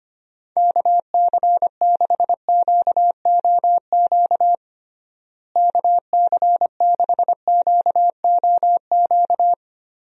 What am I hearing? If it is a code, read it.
Morse code, heard as a keyed tone, about 25 wpm, 700 Hz. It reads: KC6QOQ KC6QOQ
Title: USA Call Signs - 6 Characters - Repeated 3x - 25wpm